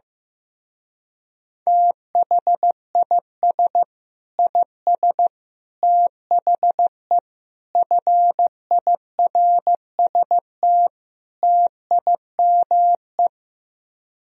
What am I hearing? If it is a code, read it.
Morse code, heard as a keyed tone, about 15 wpm, 700 Hz. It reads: THIS IS THE FIRST TIME
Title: Speed-Racing - Sets of 5 Words 15wpm